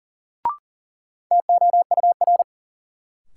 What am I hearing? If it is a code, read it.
Morse code, heard as a keyed tone, about 40 wpm, 700 Hz. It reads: TOUR
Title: Sentences from Top 1400 Words - New Word Drill Alt - 40wpm